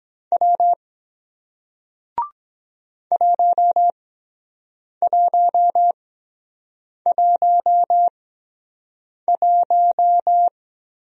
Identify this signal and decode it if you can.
Morse code, heard as a keyed tone, about 23 wpm, spaced as for 7 wpm, 700 Hz. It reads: W 1111